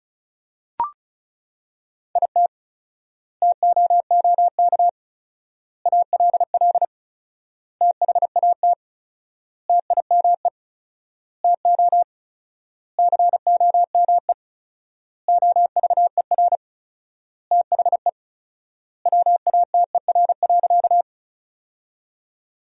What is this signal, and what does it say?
Morse code, heard as a keyed tone, about 34 wpm, 700 Hz. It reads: IT TOOK ALL THAT TIME TO COME OVER THE WATER.
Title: 4x Word-Spacing - Sentences from Top 300 Words - 35wpm